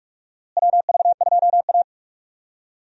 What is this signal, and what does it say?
Morse code, heard as a keyed tone, about 43 wpm, 700 Hz. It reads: WV2A
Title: USA Call Signs - 4 Characters - Repeated 2x - 45wpm